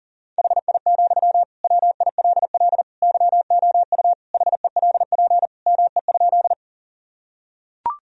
Morse code, heard as a keyed tone, about 40 wpm, 700 Hz. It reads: HI, WILL YOU HELP ME?
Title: Sentences from Top 1000 Words - Review Part 3 - 40wpm